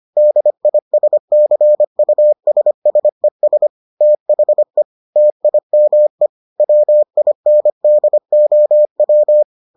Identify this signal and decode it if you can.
Morse code, heard as a keyed tone, about 25 wpm, 600 Hz. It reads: DISCUSSES THE TIME WINDOW